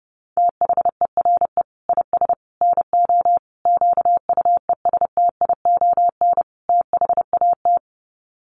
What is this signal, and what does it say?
Morse code, heard as a keyed tone, about 28 wpm, 700 Hz. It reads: THERE IS NO QUESTION THAT
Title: Sets of 5 Words Encore 30wpm